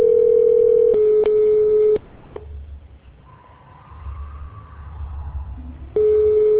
Ringback tone (Europe, 425 Hz).